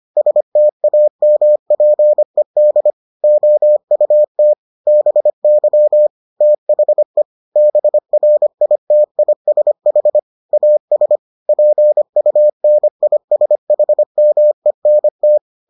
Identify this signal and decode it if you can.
Morse code, heard as a keyed tone, about 25 wpm, 600 Hz. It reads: STAMPED OUT BY THE BRITISH AS PUNISHMENT